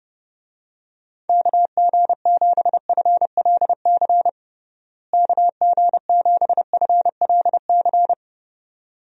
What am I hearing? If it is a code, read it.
Morse code, heard as a keyed tone, about 30 wpm, 700 Hz. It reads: KG7FLC KG7FLC
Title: USA Call Signs - 6 Characters - Repeated 2x - 30wpm